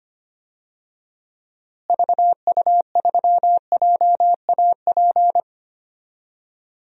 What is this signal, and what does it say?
Morse code, heard as a keyed tone, about 25 wpm, 700 Hz. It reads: VU3JAP
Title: DXCC-100 Call Signs - 6 Characters - Repeated 3x - 25wpm